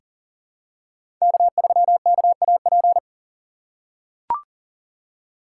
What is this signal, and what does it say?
Morse code, heard as a keyed tone, about 38 wpm, 700 Hz. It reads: K3KAP
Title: USA Call Signs - 3x Character Spacing - 40wpm